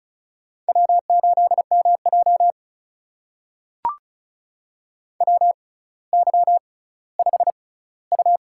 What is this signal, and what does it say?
Morse code, heard as a keyed tone, about 36 wpm, spaced as for 18 wpm, 700 Hz. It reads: W8MJ WY5U
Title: USA Call Signs - 4 Characters - 6x Character Spacing - 35wpm